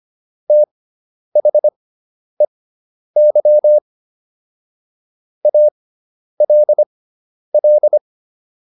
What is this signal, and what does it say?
Morse code, heard as a keyed tone, about 25 wpm, spaced as for 10 wpm, 600 Hz. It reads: THEY ALL